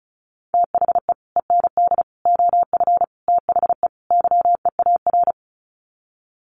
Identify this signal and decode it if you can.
Morse code, heard as a keyed tone, about 34 wpm, 700 Hz. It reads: THE END OF THE YEAR
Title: Sets of 5 Words 35wpm